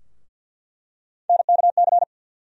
Morse code, heard as a keyed tone, about 53 wpm, 700 Hz. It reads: NKC
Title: USA Call Sign Postfixes - 3 Letters 50wpm